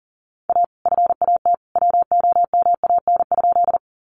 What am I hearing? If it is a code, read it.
Morse code, heard as a keyed tone, about 40 wpm, 700 Hz. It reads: A FAT WOMAN?
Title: Sentences from Top 1100 Words 40wpm